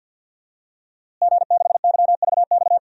Morse code, heard as a keyed tone, about 51 wpm, 700 Hz. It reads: G6YVX
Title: DXCC-100 Call Signs - 5 Characters - Repeated 3x - 50wpm